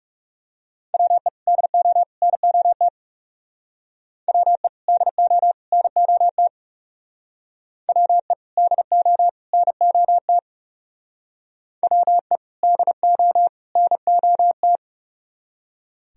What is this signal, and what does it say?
Morse code, heard as a keyed tone, about 37 wpm, 700 Hz. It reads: WE DO NOT WE DO NOT WE DO NOT WE DO NOT
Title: Speed-Racing - Sets of 3 Words 30wpm